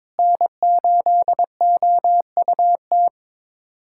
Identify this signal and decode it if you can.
Morse code, heard as a keyed tone, about 22 wpm, 700 Hz. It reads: N8OUT
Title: USA Callsign DE Callsign - 5 Characters - 22wpm